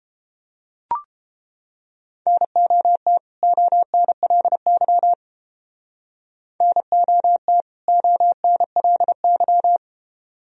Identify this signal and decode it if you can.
Morse code, heard as a keyed tone, about 29 wpm, 700 Hz. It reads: NOT ONLY NOT ONLY